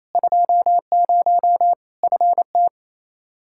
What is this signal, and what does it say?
Morse code, heard as a keyed tone, about 28 wpm, 700 Hz. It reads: 20 FT
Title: Practice QSO - Exchange 2 - Info - 28wpm